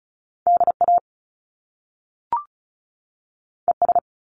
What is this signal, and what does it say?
Morse code, heard as a keyed tone, about 35 wpm, 700 Hz. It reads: D A E S
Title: Top Two Letter Combinations in Words 35wpm